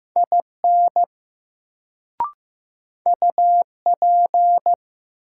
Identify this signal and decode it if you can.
Morse code, heard as a keyed tone, about 15 wpm, 700 Hz. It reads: IN UP
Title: Sentences from Top 100 Words - New Word Drill - 15wpm